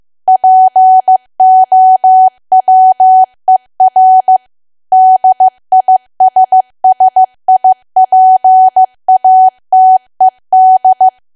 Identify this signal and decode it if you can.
Morse code, heard as a keyed tone, about 15 wpm, 750 Hz. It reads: POWER DISSIPATED
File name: cwa / W1AW-Level2 / 15WPM